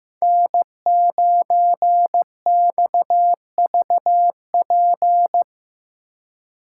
Morse code, heard as a keyed tone, about 18 wpm, 700 Hz. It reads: N9XVP
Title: Speed Racing - USA Call Signs 15wpm